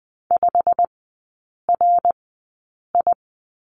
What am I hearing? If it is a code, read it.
Morse code, heard as a keyed tone, about 20 wpm, spaced as for 8 wpm, 690 Hz. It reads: 5RI
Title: 087 Lesson 15e - Number 5 - Three Characters 20wpm